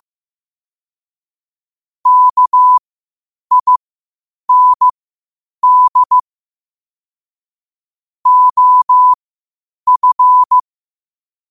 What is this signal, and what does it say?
Morse code, heard as a keyed tone, about 15 wpm, spaced as for 8 wpm, 1000 Hz. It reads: KIND OF
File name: qotdfiles / 8wpm